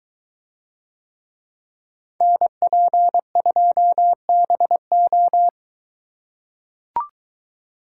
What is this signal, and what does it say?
Morse code, heard as a keyed tone, about 22 wpm, 700 Hz. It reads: NP2BO